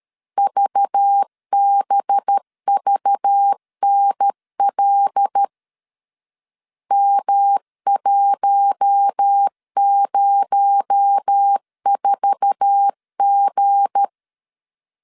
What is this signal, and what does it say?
Morse code, heard as a keyed tone, about 13 wpm, 800 Hz. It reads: VBVNL M104G